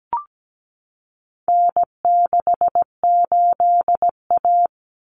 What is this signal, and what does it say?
Morse code, heard as a keyed tone, about 17 wpm, 700 Hz. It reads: N68A
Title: Common Cut Numbers - 4-Digit 17wpm